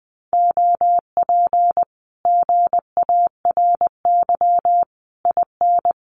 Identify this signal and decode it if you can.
Morse code, heard as a keyed tone, about 20 wpm, 700 Hz. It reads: OP GARY IN
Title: Common QSO Phrases Encore 20wpm